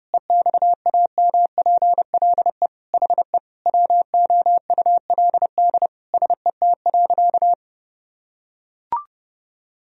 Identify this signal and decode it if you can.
Morse code, heard as a keyed tone, about 30 wpm, 700 Hz. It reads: EXAMPLE HE WOULD SET.
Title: Sentences from Top 1000 Words - Review Part 1 - 30wpm